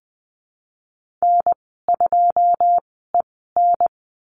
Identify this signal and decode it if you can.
Morse code, heard as a keyed tone, about 20 wpm, spaced as for 15 wpm, 700 Hz. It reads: N2EN
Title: 122 Lesson 20g - Number 3 - Call Signs 20wpm